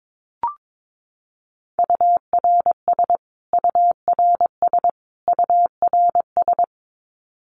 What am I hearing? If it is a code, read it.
Morse code, heard as a keyed tone, about 22 wpm, 700 Hz. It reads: URS URS URS
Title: Abbreviations Repeated 3x - 22wpm